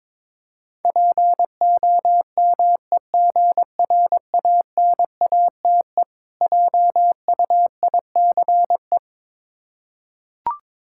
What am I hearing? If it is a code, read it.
Morse code, heard as a keyed tone, about 22 wpm, 690 Hz. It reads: POMEGRANATE JUICE
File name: Difficult Words with Hints - 22wpm